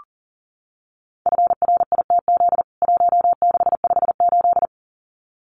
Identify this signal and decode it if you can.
Morse code, heard as a keyed tone, about 40 wpm, 700 Hz. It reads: FRITZ 1658
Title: CWT Exchange 40wpm